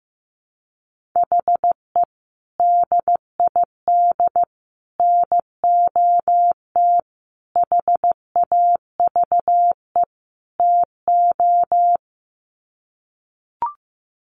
Morse code, heard as a keyed tone, about 15 wpm, 700 Hz. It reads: HE DID NOT HAVE TO
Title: Sets of 5 Words 15wpm